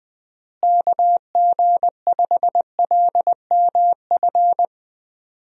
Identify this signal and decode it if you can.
Morse code, heard as a keyed tone, about 21 wpm, 700 Hz. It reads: KG5LMF
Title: USA Call Signs - 6 Characters - Repeated 2x - 20wpm